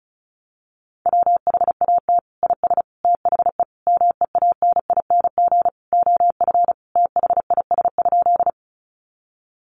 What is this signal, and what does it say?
Morse code, heard as a keyed tone, about 34 wpm, 700 Hz. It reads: WHAT IS THE MEANING OF THIS?
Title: Sentences from Top 900 Words 35wpm